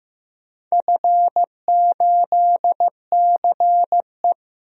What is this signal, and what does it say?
Morse code, heard as a keyed tone, about 15 wpm, 700 Hz. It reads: F8CE
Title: DXCC-100 Call Signs - 4 Characters - 15wpm